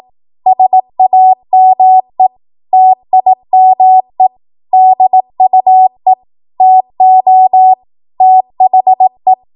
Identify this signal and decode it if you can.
Morse code, heard as a keyed tone, about 18 wpm, 750 Hz. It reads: SAME TIME DUE TO THE
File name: Morse / 18WPM